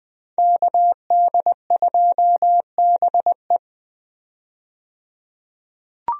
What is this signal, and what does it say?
Morse code, heard as a keyed tone, about 20 wpm, 700 Hz. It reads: KD2BE
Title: USA Call Signs - 5 Characters - 20wpm